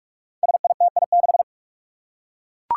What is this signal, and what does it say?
Morse code, heard as a keyed tone, about 42 wpm, 700 Hz. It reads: SITI6
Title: Five Challenging Letter-Number Combinations 45wpm